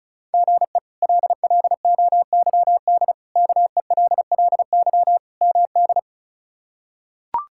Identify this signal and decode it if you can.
Morse code, heard as a keyed tone, about 36 wpm, 700 Hz. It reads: GE LLOYD KELLY MD
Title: K1USN SST QSO Practice - 35wpm